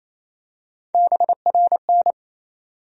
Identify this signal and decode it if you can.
Morse code, heard as a keyed tone, about 28 wpm, 700 Hz. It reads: BRN
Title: USA Call Sign Postfixes - 3 Letters 28wpm